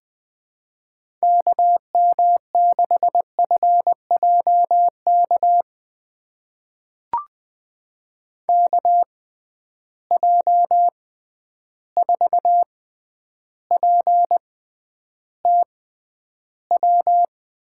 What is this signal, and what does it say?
Morse code, heard as a keyed tone, about 20 wpm, spaced as for 10 wpm, 700 Hz. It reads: KM6FJK KJ4PTW
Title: USA Call Signs - 6x Character Spacing - 20wpm